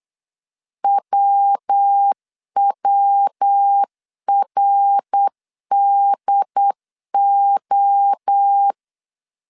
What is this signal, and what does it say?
Morse code, heard as a keyed tone, about 8 wpm, 800 Hz. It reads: WWRDO
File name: CW / DET / 1 3 1500C S B 8wpm 4141